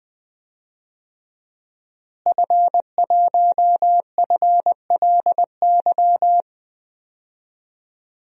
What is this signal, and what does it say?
Morse code, heard as a keyed tone, about 21 wpm, 700 Hz. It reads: F1FLY
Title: DXCC-100 Call Signs - 5 Characters - Repeated 3x - 20wpm